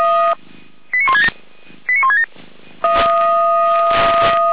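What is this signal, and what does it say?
Five-tone selective call (ZVEI), heard as quick runs of five tones, tones of about 70 ms. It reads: 89167 89167
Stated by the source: ZVEI five-tone selective call received on an SDR.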